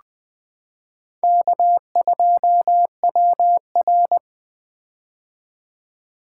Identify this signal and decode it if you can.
Morse code, heard as a keyed tone, about 20 wpm, 700 Hz. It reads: K2WR
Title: Four Letter-Number Combinations - 20wpm